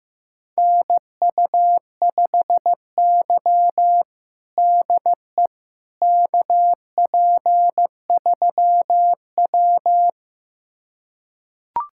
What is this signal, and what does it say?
Morse code, heard as a keyed tone, about 15 wpm, 700 Hz. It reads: NU5Y DE KP3W
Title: USA Callsign DE Callsign - 4 Characters - 15wpm